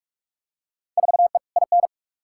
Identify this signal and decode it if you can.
Morse code, heard as a keyed tone, about 46 wpm, 700 Hz. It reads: 4E IN
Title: Field Day Exchanges - 45wpm